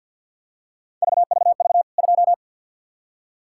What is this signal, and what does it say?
Morse code, heard as a keyed tone, about 46 wpm, 700 Hz. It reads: VVV 2
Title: K1USN SST QSO Practice 50wpm